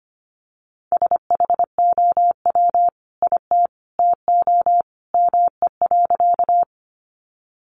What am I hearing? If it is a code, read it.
Morse code, heard as a keyed tone, about 25 wpm, 700 Hz. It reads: SHOW IT TO ME.
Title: Sentences from Top 200 Words 25wpm